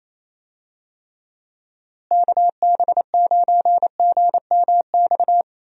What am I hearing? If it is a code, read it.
Morse code, heard as a keyed tone, about 28 wpm, 700 Hz. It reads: KB9GMX
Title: USA Call Signs - 6 Characters - Repeated 3x - 28wpm